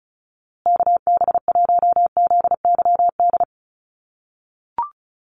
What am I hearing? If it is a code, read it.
Morse code, heard as a keyed tone, about 34 wpm, 700 Hz. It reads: KB1ZYD
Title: USA Call Signs Repeated 3x 35wpm